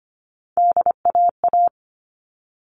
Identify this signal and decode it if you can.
Morse code, heard as a keyed tone, about 25 wpm, 700 Hz. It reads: DAA
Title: All Cut Numbers - 3-Digit 25wpm